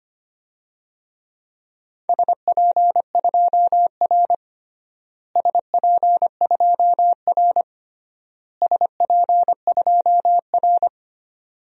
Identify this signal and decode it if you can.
Morse code, heard as a keyed tone, about 25 wpm, 700 Hz. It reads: SP2R SP2R SP2R
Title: DXCC-100 Call Signs - 4 Characters - Repeated 3x - 25wpm